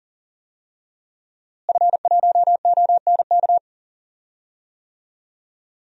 Morse code, heard as a keyed tone, about 40 wpm, 700 Hz. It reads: F1ONK
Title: DXCC-100 Call Signs - 5 Characters - Repeated 3x - 40wpm